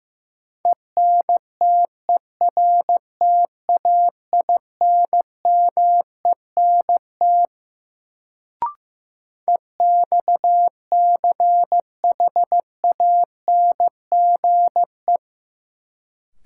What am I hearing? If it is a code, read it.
Morse code, heard as a keyed tone, about 15 wpm, 700 Hz. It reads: ENTERTAINMENT EXCHANGE
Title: Sentences from Top 1000 Words - Review Part 6 - New Word Drill - 15wpm